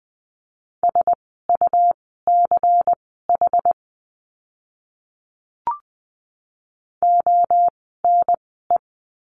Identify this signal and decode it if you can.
Morse code, heard as a keyed tone, about 19 wpm, spaced as for 14 wpm, 700 Hz. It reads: SUCH ONE
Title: 230 Lesson 37f - Letter x - Words 20wpm